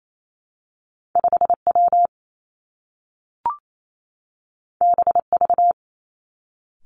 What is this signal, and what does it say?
Morse code, heard as a keyed tone, about 28 wpm, 700 Hz. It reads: 5 W B V
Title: Two Challenging Letter-Number Combinations 28wpm